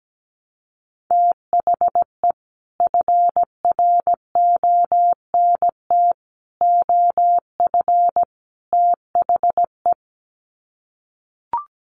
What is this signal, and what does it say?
Morse code, heard as a keyed tone, about 17 wpm, 700 Hz. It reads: THE FRONT OF THE
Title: Sets of 4 Words 17wpm